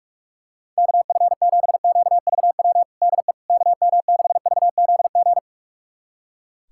Morse code, heard as a keyed tone, about 41 wpm, 690 Hz. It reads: KF7QVW DE KM6VZG
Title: USA Callsign DE Callsign 45wpm